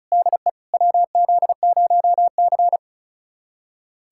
Morse code, heard as a keyed tone, about 36 wpm, 700 Hz. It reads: DE WZ0C
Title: USA Callsign DE Callsign - 4 Characters - 35wpm